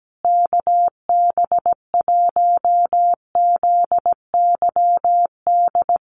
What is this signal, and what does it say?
Morse code, heard as a keyed tone, about 17 wpm, 700 Hz. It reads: KB1ZYD
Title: USA Call Signs Repeated 3x 17wpm